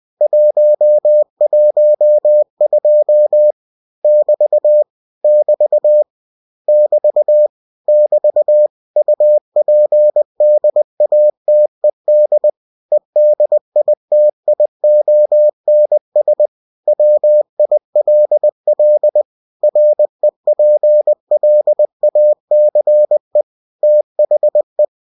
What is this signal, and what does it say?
Morse code, heard as a keyed tone, about 20 wpm, 600 Hz. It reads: 112 <BT> <BT> <BT> <BT> UPDATED EDITIONS WILL REPLACE THE
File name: cw-books / winnie-the-pooh / winnie-20wpm-page-0112